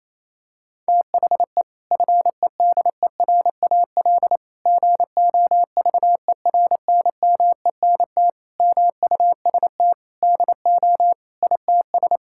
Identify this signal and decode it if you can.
Morse code, heard as a keyed tone, about 28 wpm, 700 Hz. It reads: THE FEDERAL GOVERNMENT MUST DO ITS
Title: Sentences from Top 1000 Words - Review Part 5 - 28wpm